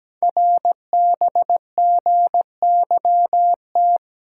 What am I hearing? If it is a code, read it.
Morse code, heard as a keyed tone, about 17 wpm, 700 Hz. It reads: RBGYT
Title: Code Groups - 5 Letters - 17wpm